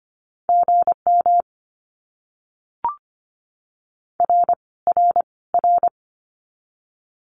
Morse code, heard as a keyed tone, about 25 wpm, 700 Hz. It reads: GM R R R
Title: Abbreviations Repeated 3x - 25wpm